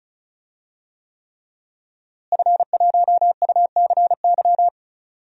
Morse code, heard as a keyed tone, about 36 wpm, 700 Hz. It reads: F1UCY
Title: DXCC-100 Call Signs - 5 Characters - Repeated 3x - 35wpm